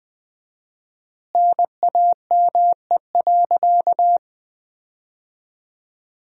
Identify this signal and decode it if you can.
Morse code, heard as a keyed tone, about 20 wpm, 700 Hz. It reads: NAME.
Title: 4x Word-Spacing - Sentences from Top 200 Words - 20wpm